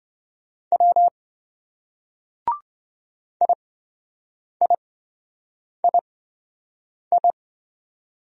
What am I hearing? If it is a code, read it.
Morse code, heard as a keyed tone, about 26 wpm, spaced as for 7 wpm, 700 Hz. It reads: W IIII